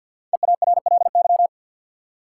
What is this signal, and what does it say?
Morse code, heard as a keyed tone, about 49 wpm, 700 Hz. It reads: EARLY
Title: Sentences from Top 400 Words - New Word Drill - 50wpm